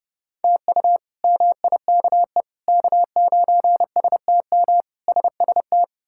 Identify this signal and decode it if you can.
Morse code, heard as a keyed tone, about 32 wpm, 700 Hz. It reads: TU MIKE K9STM SST
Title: K1USN SST QSO Practice - 30wpm